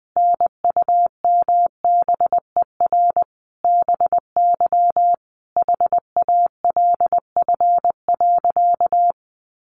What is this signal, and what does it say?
Morse code, heard as a keyed tone, about 20 wpm, 700 Hz. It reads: NUMBER BY HALF.
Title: Sentences from Top 1100 Words 20wpm